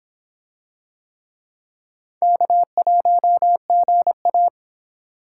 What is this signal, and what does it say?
Morse code, heard as a keyed tone, about 25 wpm, 700 Hz. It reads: K1GA